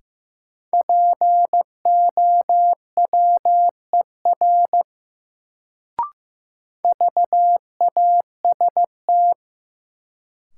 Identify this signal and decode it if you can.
Morse code, heard as a keyed tone, about 15 wpm, 700 Hz. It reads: POWER VAST
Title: Sets of 4 Words - New Word Drill - 15wpm